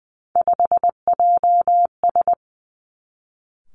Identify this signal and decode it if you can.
Morse code, heard as a keyed tone, about 19 wpm, 700 Hz. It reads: 5JS
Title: Three Challenging Letter-Number Combinations 20wpm